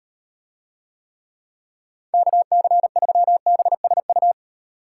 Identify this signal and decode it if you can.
Morse code, heard as a keyed tone, about 38 wpm, 700 Hz. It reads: KC3BSU